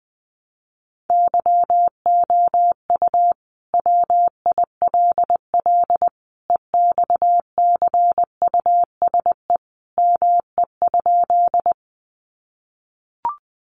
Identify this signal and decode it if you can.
Morse code, heard as a keyed tone, about 20 wpm, 700 Hz. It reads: YOU WILL EXCUSE ME?
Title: Sentences from Top 700 Words 20wpm